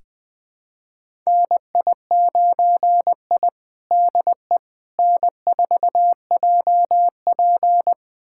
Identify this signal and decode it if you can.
Morse code, heard as a keyed tone, about 20 wpm, 700 Hz. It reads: NI9I DE N4JP
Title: USA Callsign DE Callsign - 4 Characters - 20wpm